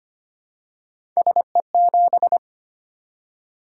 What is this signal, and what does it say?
Morse code, heard as a keyed tone, about 25 wpm, 700 Hz. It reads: SE7
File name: Three Letter-Number Combinations - 25wpm